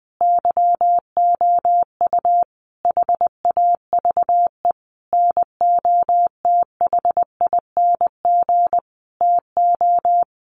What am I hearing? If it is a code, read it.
Morse code, heard as a keyed tone, about 20 wpm, 700 Hz. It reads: YOU HAVE NOTHING TO
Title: Sentences from Top 700 Words 20wpm